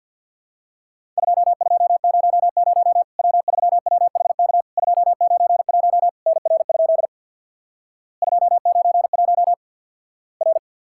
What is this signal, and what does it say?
Morse code, heard as a keyed tone, about 50 wpm, 700 Hz. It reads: 2200 W3WHK 291 NR? 291 R
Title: Worked All Europe DX Contest - QTC Exchange with Repeats - 50wpm